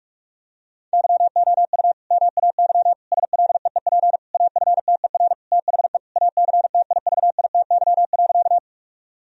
Morse code, heard as a keyed tone, about 46 wpm, 700 Hz. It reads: YOU MAY SLEEP AFTER THE ACTIVITY.
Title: Sentences from Top 1000 Words - Review Part 4 - 45wpm